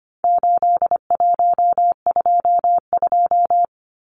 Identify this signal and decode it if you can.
Morse code, heard as a keyed tone, about 25 wpm, 700 Hz. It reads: 8122
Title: Numbers - 4-Digit 25wpm